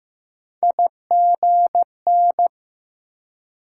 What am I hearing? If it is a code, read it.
Morse code, heard as a keyed tone, about 20 wpm, 700 Hz. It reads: IGN